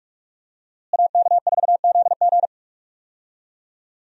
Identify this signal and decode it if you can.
Morse code, heard as a keyed tone, about 46 wpm, 700 Hz. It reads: AK4ZG
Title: USA Call Signs - 5 Characters - Repeated 2x - 45wpm